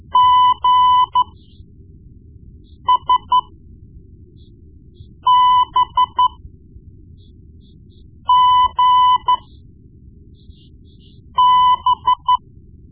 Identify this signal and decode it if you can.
Morse code, heard as a keyed tone, about 13 wpm, spaced as for 4 wpm, 1000 Hz. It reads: GSBGB
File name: Morse Lessons / Morse Lessons / Tape-1B